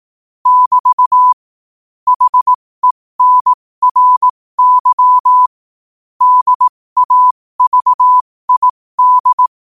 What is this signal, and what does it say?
Morse code, heard as a keyed tone, about 18 wpm, spaced as for 15 wpm, 1000 Hz. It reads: <BT> HENRY DAVID
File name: qotdfiles / 15wpm